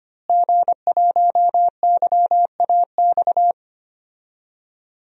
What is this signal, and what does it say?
Morse code, heard as a keyed tone, about 25 wpm, 700 Hz. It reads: G1YAX
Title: DXCC-100 Call Signs - 5 Characters - Repeated 3x - 25wpm